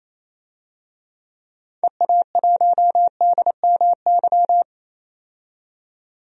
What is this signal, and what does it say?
Morse code, heard as a keyed tone, about 27 wpm, 700 Hz. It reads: EA1DMY
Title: DXCC-100 Call Signs - 6 Characters - 28wpm